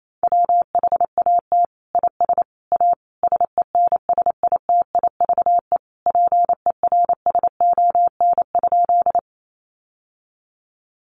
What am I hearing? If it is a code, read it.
Morse code, heard as a keyed tone, about 28 wpm, 700 Hz. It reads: WHAT IS A SENSITIVE PERSON?
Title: Sentences from Top 1400 Words 28wpm